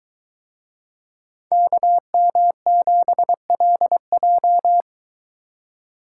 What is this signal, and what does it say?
Morse code, heard as a keyed tone, about 22 wpm, 700 Hz. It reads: KM7LJ